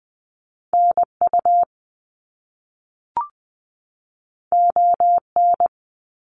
Morse code, heard as a keyed tone, about 20 wpm, 700 Hz. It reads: NU ON
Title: Canadian Provinces and Territories 20wpm